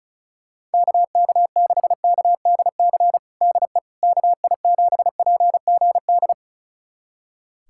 Morse code, heard as a keyed tone, about 36 wpm, 700 Hz. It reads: KK6KDC DE KI7PGD
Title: USA Callsign DE Callsign 35wpm